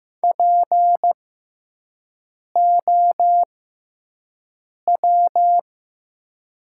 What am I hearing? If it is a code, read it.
Morse code, heard as a keyed tone, about 15 wpm, spaced as for 5 wpm, 700 Hz. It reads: POW
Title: 6x Character Spacing - Top 500 Words - 15wpm